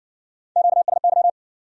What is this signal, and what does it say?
Morse code, heard as a keyed tone, about 60 wpm, 700 Hz. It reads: QSY